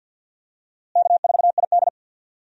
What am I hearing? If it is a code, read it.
Morse code, heard as a keyed tone, about 53 wpm, 700 Hz. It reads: K4ID
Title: USA Call Signs - 4 Characters - Repeated 2x - 50wpm